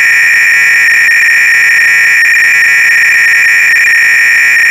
300-baud modem data.
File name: Repository / ACR-Audio-Cassette-Programs / MITS-BASIC-GAME-BATTLE-CLOADC